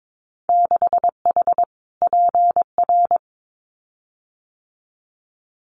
Morse code, heard as a keyed tone, about 22 wpm, 700 Hz. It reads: 6H PR
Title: Winter Field Day Exchanges Encore 22wpm